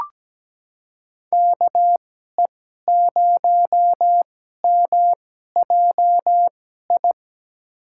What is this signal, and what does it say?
Morse code, heard as a keyed tone, about 17 wpm, spaced as for 12 wpm, 700 Hz. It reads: KE0MJI
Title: USA Call Signs - 6 Characters - 2x Character Spacing - 17wpm